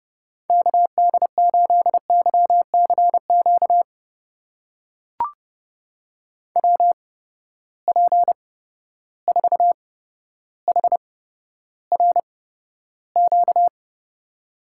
Morse code, heard as a keyed tone, about 31 wpm, spaced as for 13 wpm, 700 Hz. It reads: KD8YCQ WP4HRQ
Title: USA Call Signs - 6 Characters - 8x Character Spacing - 30wpm